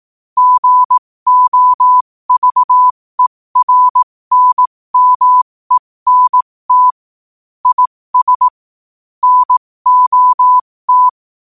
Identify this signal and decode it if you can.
Morse code, heard as a keyed tone, about 18 wpm, spaced as for 15 wpm, 1000 Hz. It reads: GOVERNMENT IS NOT
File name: qotdfiles / 15wpm